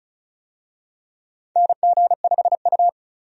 Morse code, heard as a keyed tone, about 36 wpm, 700 Hz. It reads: NG5U
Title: USA Call Signs - 4 Characters - 35wpm